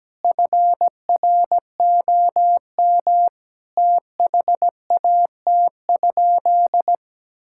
Morse code, heard as a keyed tone, about 17 wpm, 700 Hz. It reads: FROM THAT?
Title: Sentences from Top 1000 Words - Review Part 6 - 17wpm